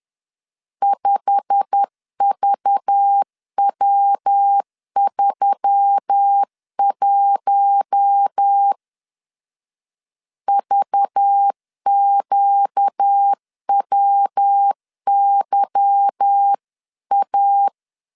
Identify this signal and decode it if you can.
Morse code, heard as a keyed tone, about 10 wpm, 800 Hz. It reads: 5VW31 VQWYA